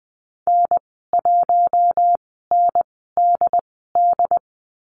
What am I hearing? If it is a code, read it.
Morse code, heard as a keyed tone, about 20 wpm, spaced as for 15 wpm, 700 Hz. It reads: N1NDD
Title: 069 Lesson 12g - Letter d - Call Signs 20wpm